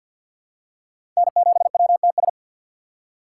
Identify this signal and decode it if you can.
Morse code, heard as a keyed tone, about 52 wpm, 700 Hz. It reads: N7WTS
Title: USA Call Signs - 5 Characters - 4x Character Spacing - 50wpm